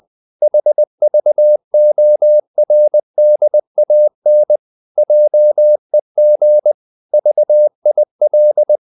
Morse code, heard as a keyed tone, about 20 wpm, 600 Hz. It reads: HVORDAN JEG VIL
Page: Ferie 20 wpm